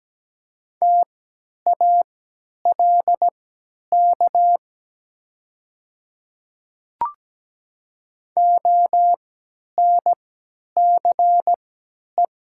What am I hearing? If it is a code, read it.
Morse code, heard as a keyed tone, about 17 wpm, spaced as for 10 wpm, 700 Hz. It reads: TALK ONCE
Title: 3x Character Spacing - Top 500 Words - 17wpm